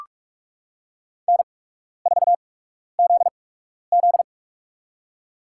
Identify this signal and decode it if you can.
Morse code, heard as a keyed tone, about 46 wpm, spaced as for 12 wpm, 700 Hz. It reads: N4ZZ
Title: USA Call Signs - 4 Characters - 8x Character Spacing - 45wpm